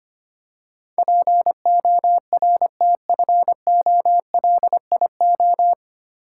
Morse code, heard as a keyed tone, about 25 wpm, 700 Hz. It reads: PORTFOLIO
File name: practice / 25wpm / 9 Letter Words - First 3 Letters Given - 25wpm